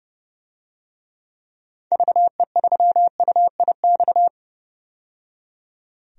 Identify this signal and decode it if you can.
Morse code, heard as a keyed tone, about 31 wpm, 700 Hz. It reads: VE3UIX
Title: DXCC-100 Call Signs - 6 Characters - 30wpm